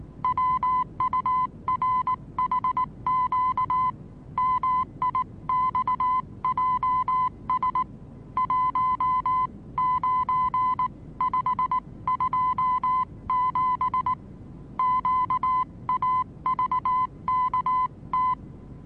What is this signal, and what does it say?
Morse code, heard as a keyed tone, about 19 wpm, 1010 Hz. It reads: WURHQ MIXJS 19527 QAVKT
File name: repos / Morsekurs / DJ4UF Morsen26